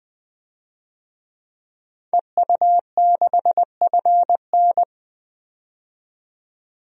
Morse code, heard as a keyed tone, about 20 wpm, 700 Hz. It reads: EU6FN
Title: DXCC-100 Call Signs - 5 Characters - 20wpm